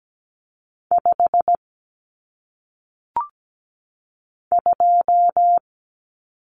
Morse code, heard as a keyed tone, about 17 wpm, 700 Hz. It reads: 5 2
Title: Numbers - 1-Digit 17wpm